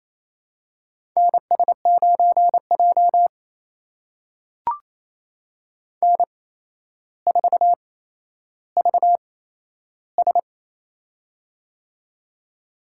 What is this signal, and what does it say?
Morse code, heard as a keyed tone, about 28 wpm, spaced as for 12 wpm, 700 Hz. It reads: NS9J N4VS
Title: USA Call Signs - 4 Characters - 8x Character Spacing - 28wpm